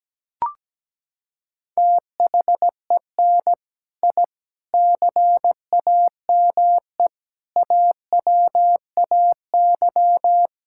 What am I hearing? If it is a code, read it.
Morse code, heard as a keyed tone, about 17 wpm, 700 Hz. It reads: THEN I CAME AWAY
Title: Sentences from Top 1000 Words - Review Part 6 - 17wpm